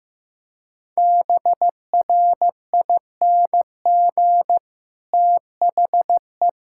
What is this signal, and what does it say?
Morse code, heard as a keyed tone, about 15 wpm, 700 Hz. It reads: BRING THE
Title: Sentences from Top 1000 Words - Review Part 6 - 15wpm